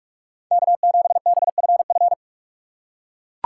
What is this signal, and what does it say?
Morse code, heard as a keyed tone, about 44 wpm, 700 Hz. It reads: K7BFF
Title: USA Call Signs - 5 Characters - 6x Character Spacing - 45wpm